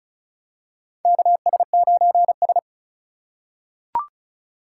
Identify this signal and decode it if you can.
Morse code, heard as a keyed tone, about 36 wpm, 690 Hz. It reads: KS9S
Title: USA Call Signs - 4 Characters - 3x Character Spacing - 35wpm